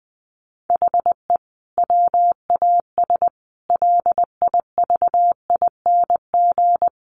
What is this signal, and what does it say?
Morse code, heard as a keyed tone, about 19 wpm, 700 Hz. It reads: HE WAS LIVING
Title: Sentences from Top 1100 Words 20wpm